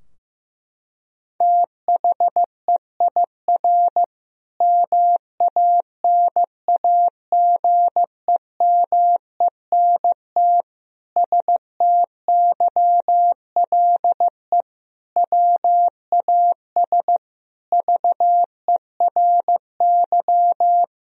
Morse code, heard as a keyed tone, about 15 wpm, 700 Hz. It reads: THEIR MANAGEMENT STYLE WAS VERY
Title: Sentences from Top 1000 Words - Review Part 4 - 15wpm